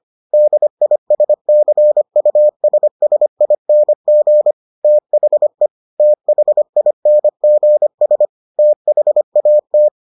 Morse code, heard as a keyed tone, about 25 wpm, 600 Hz. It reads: DISCUSSING THE THINGS THAT